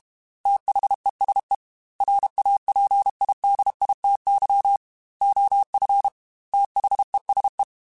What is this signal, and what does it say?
Morse code, heard as a keyed tone, about 32 wpm, 800 Hz. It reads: THESE RAPIDITY OF THESE
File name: Technical Subjects / Technical Subjects-mp3s / Reactance @30